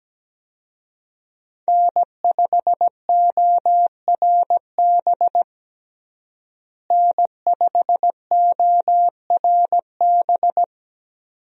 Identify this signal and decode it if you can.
Morse code, heard as a keyed tone, about 17 wpm, 700 Hz. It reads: N5ORB N5ORB
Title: USA Call Signs - 5 Characters - Repeated 2x - 17wpm